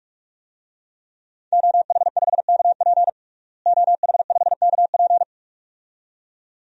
Morse code, heard as a keyed tone, about 46 wpm, 700 Hz. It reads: OH5KP OH5KP
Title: DXCC-100 Call Signs - 5 Characters - Repeated 2x - 45wpm